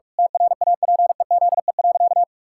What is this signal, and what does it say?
Morse code, heard as a keyed tone, about 41 wpm, 700 Hz. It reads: TRAPEZE.
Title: Pangram Challenge v2 - 45wpm